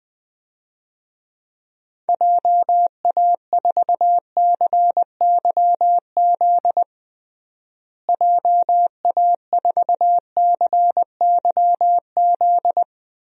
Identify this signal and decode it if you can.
Morse code, heard as a keyed tone, about 20 wpm, 700 Hz. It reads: JA4CYZ JA4CYZ
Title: DXCC-100 Call Signs - 6 Characters - Repeated 3x - 20wpm